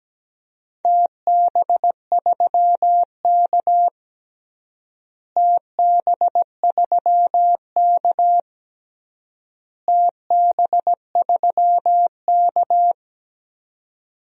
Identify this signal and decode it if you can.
Morse code, heard as a keyed tone, about 17 wpm, 700 Hz. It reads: TB3K TB3K TB3K
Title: DXCC-100 Call Signs - 4 Characters - Repeated 3x - 17wpm